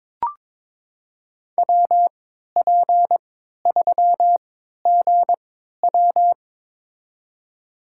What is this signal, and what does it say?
Morse code, heard as a keyed tone, about 21 wpm, spaced as for 12 wpm, 700 Hz. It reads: WP3GW
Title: USA Call Signs - 3x Character Spacing - 22wpm